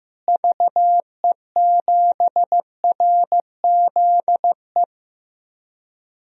Morse code, heard as a keyed tone, about 15 wpm, 700 Hz. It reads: VE7RZE
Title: DXCC-100 Call Signs - 6 Characters - Repeated 3x - 15wpm